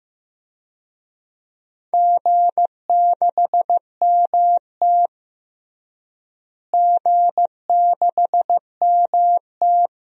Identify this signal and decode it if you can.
Morse code, heard as a keyed tone, about 15 wpm, 700 Hz. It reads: G6MT G6MT
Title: DXCC-100 Call Signs - 4 Characters - Repeated 2x - 15wpm